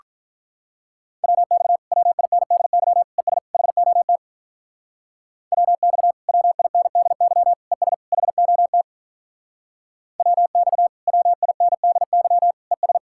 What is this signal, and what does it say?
Morse code, heard as a keyed tone, about 46 wpm, 700 Hz. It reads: WX WINDY ES HOT WX WINDY ES HOT WX WINDY ES